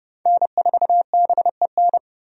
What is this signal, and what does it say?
Morse code, heard as a keyed tone, about 30 wpm, 700 Hz. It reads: N4BEN
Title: USA Call Signs - 5 Characters - 30wpm